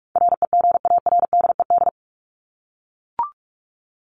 Morse code, heard as a keyed tone, about 44 wpm, 700 Hz. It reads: REGARDED
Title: ED Suffix - Top 100 Words with Base Words 45wpm